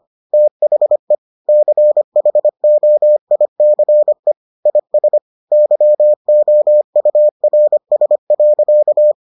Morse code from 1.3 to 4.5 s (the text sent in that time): CHOICE